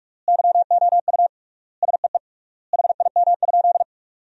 Morse code, heard as a keyed tone, about 46 wpm, 700 Hz. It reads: YOU SEE HIM?
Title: 3x Word-Spacing - Sentences from Top 100 Words - 45wpm